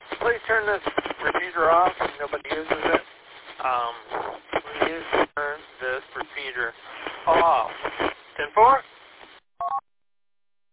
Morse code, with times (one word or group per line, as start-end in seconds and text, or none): none